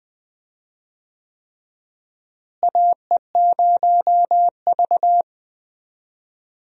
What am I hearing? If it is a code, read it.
Morse code, heard as a keyed tone, about 20 wpm, 700 Hz. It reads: AE0V
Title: USA Call Signs - 4 Characters - Repeated 3x - 20wpm